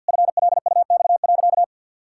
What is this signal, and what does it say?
Morse code, heard as a keyed tone, about 51 wpm, 700 Hz. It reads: FLUX.
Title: Pangram Challenge v2 - 50wpm